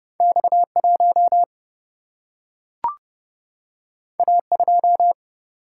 Morse code, heard as a keyed tone, about 30 wpm, 700 Hz. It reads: X1 A2
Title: Single-Letter Single-Number 30wpm